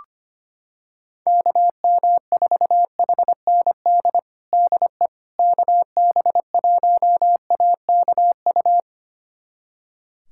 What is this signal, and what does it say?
Morse code, heard as a keyed tone, about 25 wpm, 700 Hz. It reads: KM4HND DE KB1AKU
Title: USA Callsign DE Callsign 25wpm